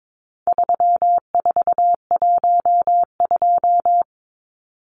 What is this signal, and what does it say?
Morse code, heard as a keyed tone, about 22 wpm, 700 Hz. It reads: 3412
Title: Numbers - 4-Digit 22wpm